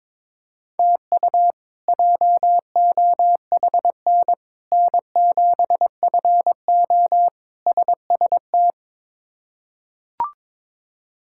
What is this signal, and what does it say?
Morse code, heard as a keyed tone, about 21 wpm, 700 Hz. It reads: TU JOHN N7FO SST
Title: K1USN SST QSO Practice - 22wpm